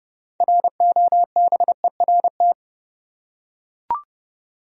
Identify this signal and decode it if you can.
Morse code, heard as a keyed tone, about 30 wpm, 700 Hz. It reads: ROBERT
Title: USA Common Names - 30wpm